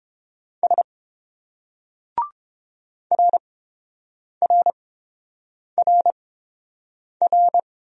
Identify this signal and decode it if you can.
Morse code, heard as a keyed tone, about 30 wpm, spaced as for 7 wpm, 700 Hz. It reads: S RRRR